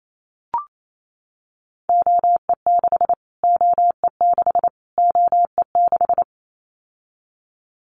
Morse code, heard as a keyed tone, about 28 wpm, 700 Hz. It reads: OE6 OE6 OE6
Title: Callsign Prefixes for Top 25 DXCC - Repeated 3x - 28wpm